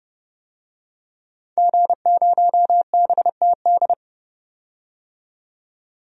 Morse code, heard as a keyed tone, about 30 wpm, 700 Hz. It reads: G0BTD
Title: DXCC-100 Call Signs - 5 Characters - 30wpm